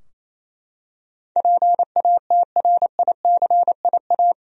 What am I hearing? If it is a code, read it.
Morse code, heard as a keyed tone, about 27 wpm, 700 Hz. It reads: PATRICIA
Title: USA Common Names - 28wpm